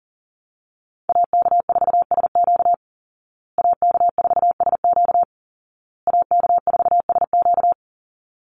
Morse code, heard as a keyed tone, about 40 wpm, 700 Hz. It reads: AK4SQ AK4SQ AK4SQ
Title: Call Signs Repeated 2x - 40wpm